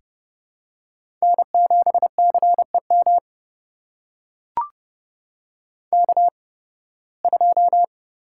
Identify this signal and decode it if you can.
Morse code, heard as a keyed tone, about 30 wpm, spaced as for 20 wpm, 700 Hz. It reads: N7CEM K2
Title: USA Call Signs - 5 Characters - 8x Character Spacing - 30wpm